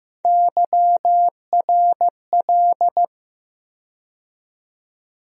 Morse code, heard as a keyed tone, about 15 wpm, 700 Hz. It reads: YRL
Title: USA Call Sign Postfixes - 3 Letters 15wpm